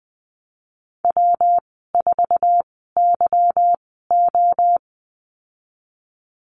Morse code, heard as a keyed tone, about 20 wpm, spaced as for 15 wpm, 700 Hz. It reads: W4YO
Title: 193 Lesson 31g - Letter v - Call Signs 20wpm